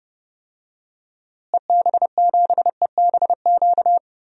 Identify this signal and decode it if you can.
Morse code, heard as a keyed tone, about 30 wpm, 700 Hz. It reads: EB7EBQ
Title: DXCC-100 Call Signs - 6 Characters - Repeated 3x - 30wpm